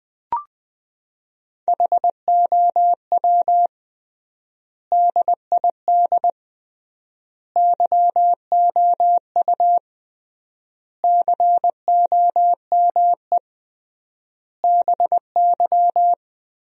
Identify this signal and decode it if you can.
Morse code, heard as a keyed tone, about 20 wpm, 700 Hz. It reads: HOW DID YOU COME BY
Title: 3x Word-Spacing - Sentences from Top 100 Words - 20wpm